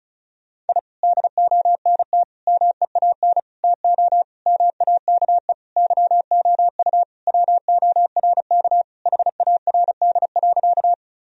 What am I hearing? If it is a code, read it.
Morse code, heard as a keyed tone, about 34 wpm, 700 Hz. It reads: I DONT MEAN TO MAKE YOU WORK HARD.
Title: Sentences from Top 1000 Words - Review Part 1 - 35wpm